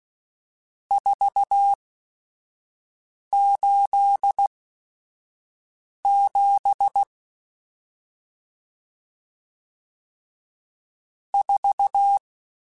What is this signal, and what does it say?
Morse code, heard as a keyed tone, about 16 wpm, spaced as for 5 wpm, 800 Hz. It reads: 487 4
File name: RSTs / RST-mp3s / Various RST @5